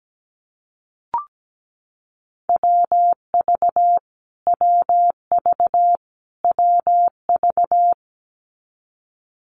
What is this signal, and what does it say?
Morse code, heard as a keyed tone, about 17 wpm, 700 Hz. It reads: WV WV WV
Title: US States Repeated 3x 17wpm